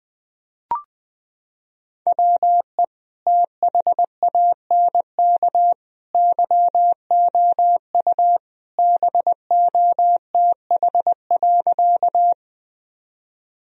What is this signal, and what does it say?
Morse code, heard as a keyed tone, about 20 wpm, 700 Hz. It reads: WE THANK YOU BOTH.
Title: Sentences from Top 1000 Words - Review Part 1 - 20wpm